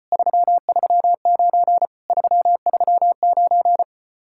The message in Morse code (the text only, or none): 339 339